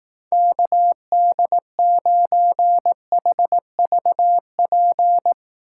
Morse code, heard as a keyed tone, about 18 wpm, 700 Hz. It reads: KD9HVP
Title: Speed Racing - USA 2x3 Call Signs 15wpm